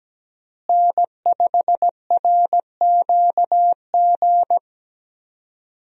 Morse code, heard as a keyed tone, about 17 wpm, 700 Hz. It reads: N5RQG
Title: USA Call Signs - 5 Characters - Repeated 3x - 17wpm